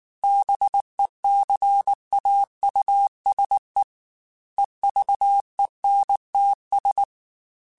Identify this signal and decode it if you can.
Morse code, heard as a keyed tone, about 19 wpm, 800 Hz. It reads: BECAUSE EVENTS